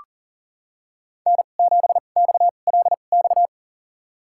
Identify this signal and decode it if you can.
Morse code, heard as a keyed tone, about 40 wpm, spaced as for 29 wpm, 700 Hz. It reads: N7XLX
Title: USA Call Signs - 2x Character Spacing - 40wpm